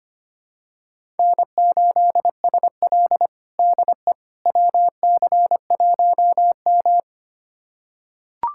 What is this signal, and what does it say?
Morse code, heard as a keyed tone, about 25 wpm, 700 Hz. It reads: N8SL DE WC1M
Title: USA Callsign DE Callsign - 4 Characters - 25wpm